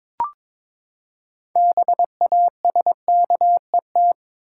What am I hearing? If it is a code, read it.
Morse code, heard as a keyed tone, about 21 wpm, 700 Hz. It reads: BASKET
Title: Six Letter Words - 22wpm